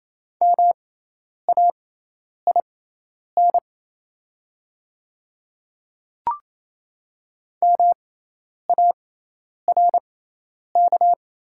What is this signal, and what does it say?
Morse code, heard as a keyed tone, about 28 wpm, spaced as for 10 wpm, 700 Hz. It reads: MAIN MARK
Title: 6x Character Spacing - Top 300 Words - 28wpm